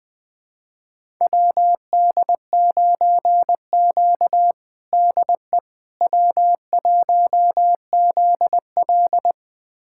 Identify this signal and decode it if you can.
Morse code, heard as a keyed tone, about 20 wpm, 700 Hz. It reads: WD9Q DE W1ZL
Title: USA Callsign DE Callsign - 4 Characters - 20wpm